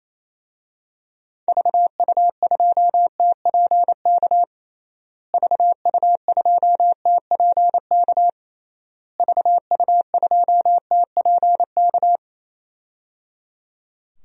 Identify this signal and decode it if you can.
Morse code, heard as a keyed tone, about 28 wpm, 700 Hz. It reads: VU2TPK VU2TPK VU2TPK
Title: DXCC-100 Call Signs - 6 Characters - Repeated 3x - 28wpm